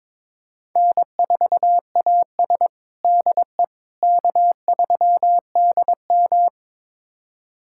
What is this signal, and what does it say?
Morse code, heard as a keyed tone, about 22 wpm, 700 Hz. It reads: N4AS DE K3DM
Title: USA Callsign DE Callsign - 4 Characters - 22wpm